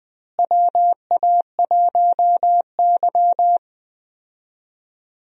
Morse code, heard as a keyed tone, about 20 wpm, 700 Hz. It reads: WA1Y
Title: USA Call Signs - 4 Characters - Repeated 2x - 20wpm